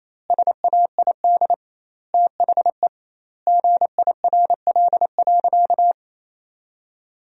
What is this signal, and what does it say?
Morse code, heard as a keyed tone, about 28 wpm, 700 Hz. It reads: SAID THE GIRL.
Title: 2x Word-Spacing - Sentences from Top 400 Words - 28wpm